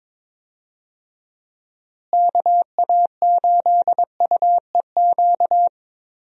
Morse code, heard as a keyed tone, about 21 wpm, 700 Hz. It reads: KA8UEQ
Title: USA Call Signs - 6 Characters - Repeated 3x - 22wpm